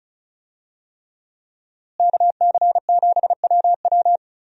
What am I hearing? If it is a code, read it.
Morse code, heard as a keyed tone, about 36 wpm, 700 Hz. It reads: KC7WW
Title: USA Call Signs - 5 Characters - Repeated 3x - 35wpm